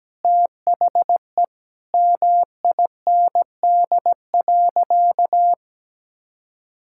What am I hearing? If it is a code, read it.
Morse code, heard as a keyed tone, about 17 wpm, 700 Hz. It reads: THE MIND.
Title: Sentences from Top 1000 Words - Review Part 6 - 17wpm